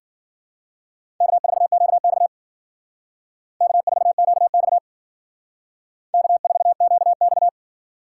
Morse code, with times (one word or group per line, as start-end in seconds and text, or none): 1.2-2.3 K4QX
3.6-4.8 K4QX
6.1-7.5 K4QX